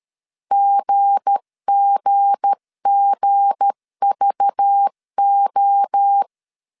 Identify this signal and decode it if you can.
Morse code, heard as a keyed tone, about 13 wpm, 800 Hz. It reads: GGGVO
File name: LEZIONI / 7 GF9Z / 7 G 12wpm 2055